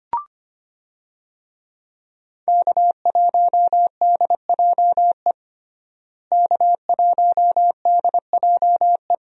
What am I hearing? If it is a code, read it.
Morse code, heard as a keyed tone, about 25 wpm, 700 Hz. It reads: K1DJE K1DJE
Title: USA Call Signs - 5 Characters - Repeated 3x - 25wpm